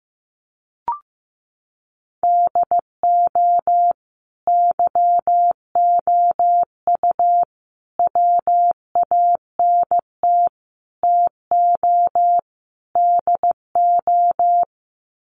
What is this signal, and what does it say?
Morse code, heard as a keyed tone, about 15 wpm, 700 Hz. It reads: DO YOU WANT TO DO
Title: Sets of 5 Words Encore 15wpm